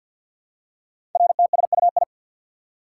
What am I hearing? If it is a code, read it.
Morse code, heard as a keyed tone, about 48 wpm, 700 Hz. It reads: RTSUI
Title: Code Groups - 5 Letters - 50wpm